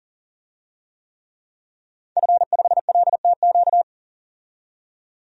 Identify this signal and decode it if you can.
Morse code, heard as a keyed tone, about 40 wpm, 700 Hz. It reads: F5LTQ
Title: DXCC-100 Call Signs - 5 Characters - 40wpm